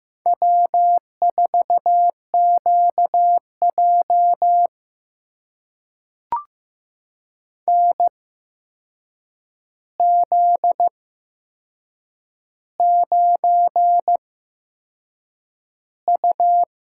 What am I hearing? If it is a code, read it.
Morse code, heard as a keyed tone, about 15 wpm, spaced as for 6 wpm, 700 Hz. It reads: W4QJ NZ9U
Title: USA Call Signs - 8x Character Spacing - 15wpm